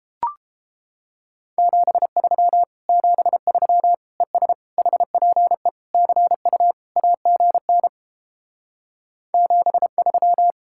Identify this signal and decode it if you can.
Morse code, heard as a keyed tone, about 29 wpm, 700 Hz. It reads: 73 73 ES HPE CU AGN 73